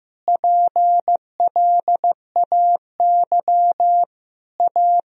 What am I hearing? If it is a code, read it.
Morse code, heard as a keyed tone, about 15 wpm, 700 Hz. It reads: PLAY A
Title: Sentences from Top 1000 Words - Review Part 4 - 15wpm